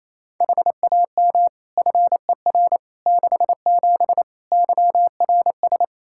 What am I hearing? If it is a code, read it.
Morse code, heard as a keyed tone, about 28 wpm, 700 Hz. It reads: HAM FER 67 YRS
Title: Practice QSO - Exchange 3 - More Info - 28wpm